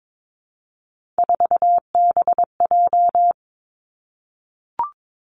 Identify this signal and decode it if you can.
Morse code, heard as a keyed tone, about 22 wpm, 700 Hz. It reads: 4BJ
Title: Three Challenging Letter-Number Combinations 22wpm